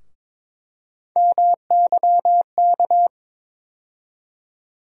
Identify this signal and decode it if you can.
Morse code, heard as a keyed tone, about 22 wpm, 700 Hz. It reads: MYK
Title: USA Call Sign Postfixes - 3 Letters 22wpm